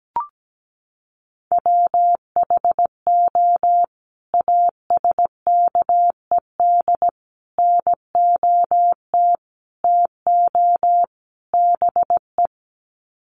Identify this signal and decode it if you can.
Morse code, heard as a keyed tone, about 17 wpm, 700 Hz. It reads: WHO ASKED NOT TO BE
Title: Sets of 5 Words Encore2 17wpm